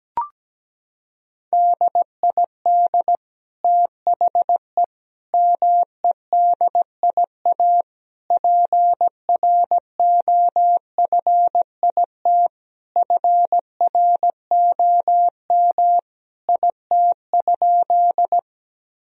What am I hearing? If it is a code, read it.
Morse code, heard as a keyed tone, about 17 wpm, 700 Hz. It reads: DID THE MEDIA PROFIT FROM IT?
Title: Sentences from Top 1000 Words - Review Part 4 - 17wpm